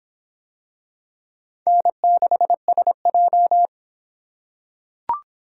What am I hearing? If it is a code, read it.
Morse code, heard as a keyed tone, about 23 wpm, 700 Hz. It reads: N6SJ